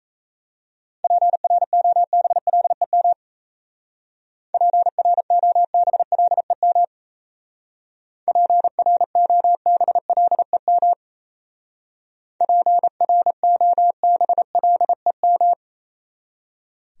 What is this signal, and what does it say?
Morse code, heard as a keyed tone, about 35 wpm, 700 Hz. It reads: PROBLEM PROBLEM PROBLEM PROBLEM